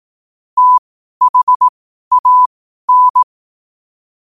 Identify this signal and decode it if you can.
Morse code, heard as a keyed tone, about 18 wpm, spaced as for 12 wpm, 1000 Hz. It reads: THAN
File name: qotdfiles / 12wpm